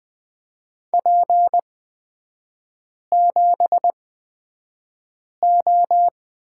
Morse code, heard as a keyed tone, about 20 wpm, spaced as for 5 wpm, 700 Hz. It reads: P7O
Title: 224 Lesson 36e - BT - Three Characters 20wpm